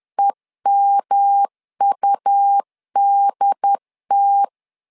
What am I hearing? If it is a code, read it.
Morse code, heard as a keyed tone, about 10 wpm, 800 Hz. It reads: EMUDT
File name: LEZIONI / 3 MWRO / 3 MW 10wpm 1959